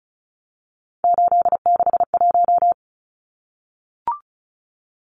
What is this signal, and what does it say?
Morse code, heard as a keyed tone, about 34 wpm, 700 Hz. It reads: 861
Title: Numbers - 3-Digit 35wpm